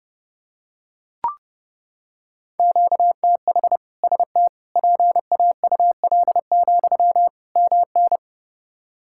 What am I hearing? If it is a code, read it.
Morse code, heard as a keyed tone, about 30 wpm, 700 Hz. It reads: QTH ST PAUL, MN
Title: K1USN SST QSO Practice 30wpm